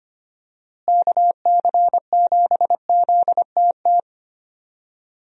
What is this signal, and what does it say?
Morse code, heard as a keyed tone, about 25 wpm, 700 Hz. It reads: KC7ZTT
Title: USA Call Signs - 6 Characters - Repeated 3x - 25wpm